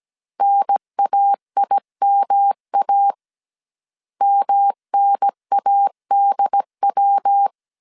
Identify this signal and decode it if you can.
Morse code, heard as a keyed tone, about 17 wpm, 800 Hz. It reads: NAIMA MNADW
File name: CW / DET / 1 3 1500C S C 16wpm 2148